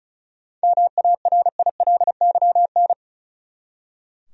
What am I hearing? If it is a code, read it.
Morse code, heard as a keyed tone, about 36 wpm, 700 Hz. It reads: MARILYN
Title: USA Common Names - 35wpm